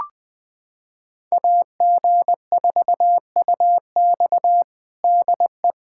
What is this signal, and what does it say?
Morse code, heard as a keyed tone, about 20 wpm, 700 Hz. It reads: AG4UX DE
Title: USA Callsign DE Callsign - 5 Characters - 20wpm